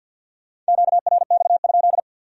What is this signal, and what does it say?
Morse code, heard as a keyed tone, about 59 wpm, 700 Hz. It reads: QRX?
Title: Speed-Racing Q-Codes - 50wpm